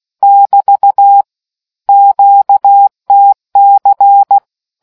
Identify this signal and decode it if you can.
Morse code, heard as a keyed tone, about 16 wpm, 800 Hz. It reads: <BT> QTC